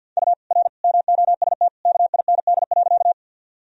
Morse code, heard as a keyed tone, about 52 wpm, 700 Hz. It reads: U R MOST KIND.
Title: Sentences from Top 200 Words and Abbreviations 50wpm